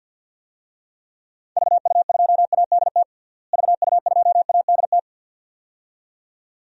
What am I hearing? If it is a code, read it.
Morse code, heard as a keyed tone, about 52 wpm, 700 Hz. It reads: VU2ADT VU2ADT
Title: DXCC-100 Call Signs - 6 Characters - Repeated 2x - 50wpm